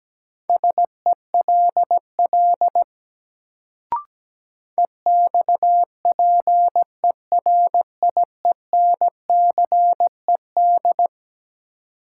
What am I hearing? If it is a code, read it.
Morse code, heard as a keyed tone, about 17 wpm, 700 Hz. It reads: SELL EXPERIENCED
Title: Sentences from Top 1100 Words - New Word Drill Alt - 17wpm